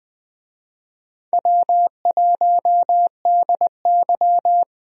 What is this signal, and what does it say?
Morse code, heard as a keyed tone, about 20 wpm, 700 Hz. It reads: W1DY
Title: USA Call Signs - 4 Characters - 20wpm